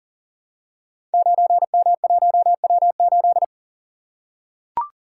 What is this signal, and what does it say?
Morse code, heard as a keyed tone, about 42 wpm, 690 Hz. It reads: 9M1W8
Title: Code Groups - 5 Letter:Numbers - 40wpm